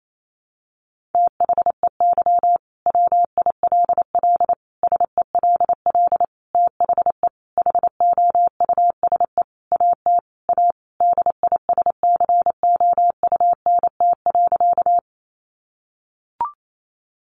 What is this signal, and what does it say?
Morse code, heard as a keyed tone, about 28 wpm, 700 Hz. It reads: THEY WILL SELL THE HOUSE AT A DISCOUNT.
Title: Sentences from Top 1200 Words 28wpm